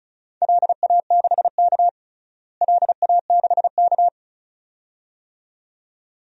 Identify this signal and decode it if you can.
Morse code, heard as a keyed tone, about 34 wpm, 700 Hz. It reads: LA6K LA6K
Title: DXCC-100 Call Signs - 4 Characters - Repeated 2x - 35wpm